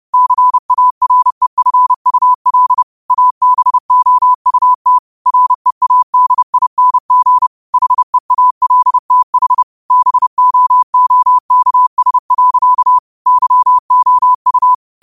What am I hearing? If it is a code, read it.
Morse code, heard as a keyed tone, about 30 wpm, 1000 Hz. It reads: CAREFUL ABOUT READING HEALTH BOOKS. YOU
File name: qotdfiles / 30wpm